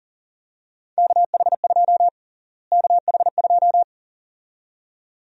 Morse code, heard as a keyed tone, about 40 wpm, 700 Hz. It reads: KH2 KH2
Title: USA Call Sign Prefixes - Repeated 2x - 40wpm